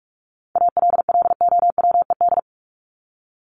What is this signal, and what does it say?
Morse code, heard as a keyed tone, about 44 wpm, 700 Hz. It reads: ALLOWED
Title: ED Suffix - Top 100 Words with Base Words 45wpm